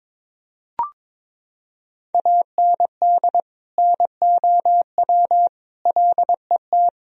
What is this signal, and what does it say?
Morse code, heard as a keyed tone, about 22 wpm, 700 Hz. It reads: AND NOW LET
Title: Sentences from Top 1000 Words - Review Part 2 - 22wpm